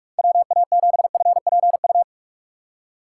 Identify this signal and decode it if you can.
Morse code, heard as a keyed tone, about 46 wpm, 700 Hz. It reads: WA7FPU
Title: USA Call Signs - 6 Characters - 3x Character Spacing - 45wpm